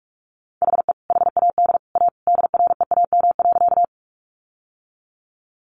Morse code, heard as a keyed tone, about 45 wpm, 700 Hz. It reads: HE HAD A DREAM.
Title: Sentences from Top 1500 Words 45wpm